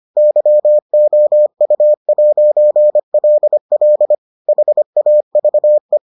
Morse code, heard as a keyed tone, about 25 wpm, 600 Hz. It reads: YOU'LL HAVE